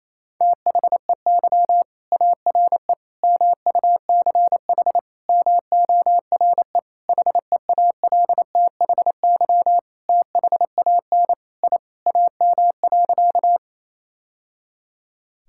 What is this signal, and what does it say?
Morse code, heard as a keyed tone, about 28 wpm, 700 Hz. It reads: THEY ARE MUCH MORE HEALTHY THAN I AM.
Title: Sentences from Top 1000 Words - Review Part 5 - 28wpm